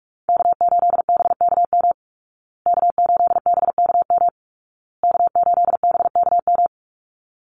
Morse code, heard as a keyed tone, about 44 wpm, 700 Hz. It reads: K8BKM K8BKM K8BKM
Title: Call Signs Repeated 2x - 45wpm